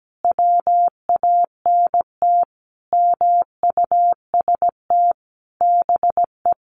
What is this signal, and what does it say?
Morse code, heard as a keyed tone, about 17 wpm, 700 Hz. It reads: WANT MUST BE
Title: Sentences from Top 1400 Words 17wpm